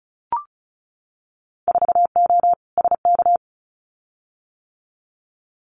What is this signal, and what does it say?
Morse code, heard as a keyed tone, about 36 wpm, 700 Hz. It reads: 4O SK
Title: Winter Field Day Exchanges Encore 35wpm